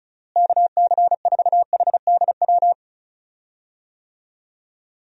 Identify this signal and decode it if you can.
Morse code, heard as a keyed tone, about 36 wpm, 700 Hz. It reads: KC4HDW
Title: USA Call Signs - 6 Characters - Repeated 2x - 35wpm